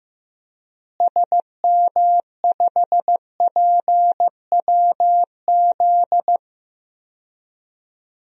Morse code, heard as a keyed tone, about 15 wpm, 700 Hz. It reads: SM5PWZ
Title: DXCC-100 Call Signs - 6 Characters - Repeated 3x - 15wpm